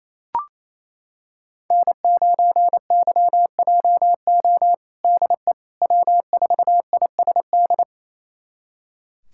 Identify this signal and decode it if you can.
Morse code, heard as a keyed tone, about 28 wpm, 700 Hz. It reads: N9YJO DE W4ISD
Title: USA Callsign DE Callsign - 5 Characters - 28wpm